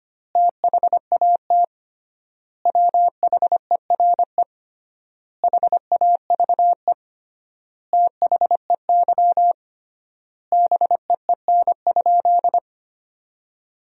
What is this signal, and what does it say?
Morse code, heard as a keyed tone, about 25 wpm, 700 Hz. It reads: THAT WHERE HAVE THEY BEEN?
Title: 3x Word-Spacing - Sentences from Top 200 Words - 25wpm